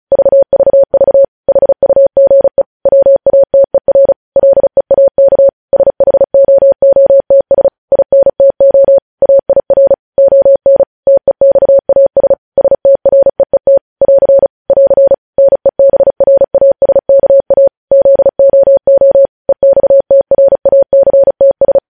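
Morse code, heard as a keyed tone, about 36 wpm, 550 Hz. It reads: VVV HUGE WATER LEAK SHOOTS INTO AIR ON TEXAS STREET <AR> <AR> NEBRASKA ZOO EXTRACTS
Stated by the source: Morse news headlines practice at 35.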